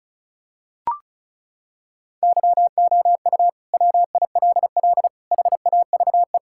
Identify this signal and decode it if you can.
Morse code, heard as a keyed tone, about 36 wpm, 700 Hz. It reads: YOU WILL HAVE
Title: Sentences from Top 1600 Words 35wpm